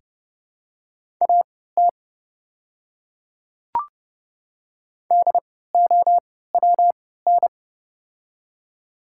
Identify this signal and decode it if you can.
Morse code, heard as a keyed tone, about 30 wpm, spaced as for 17 wpm, 700 Hz. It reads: AT DOWN
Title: 3x Character Spacing - Top 100 Words - 30wpm